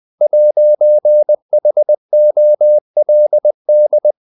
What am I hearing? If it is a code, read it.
Morse code, heard as a keyed tone, about 20 wpm, 600 Hz. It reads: 'HOLD
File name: CW / Practice / If20